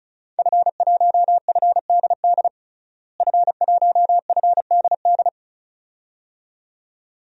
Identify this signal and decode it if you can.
Morse code, heard as a keyed tone, about 36 wpm, 700 Hz. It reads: F1FDD F1FDD
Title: DXCC-100 Call Signs - 5 Characters - Repeated 2x - 35wpm